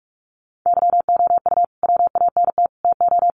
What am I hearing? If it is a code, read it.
Morse code, heard as a keyed tone, about 44 wpm, 700 Hz. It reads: YOU WANT TO
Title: Sets of 4 Words 45wpm